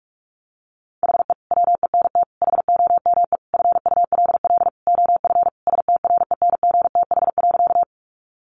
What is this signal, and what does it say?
Morse code, heard as a keyed tone, about 46 wpm, 700 Hz. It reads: HE WENT HOME FULL OF STRENGTH.
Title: Sentences from Top 1300 Words 45wpm